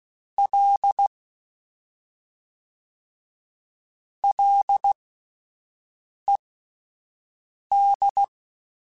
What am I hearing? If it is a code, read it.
Morse code, heard as a keyed tone, about 16 wpm, spaced as for 5 wpm, 800 Hz. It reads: L LED
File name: Ls / Single L m3s / Single L Prefixe 3x2 Twice @5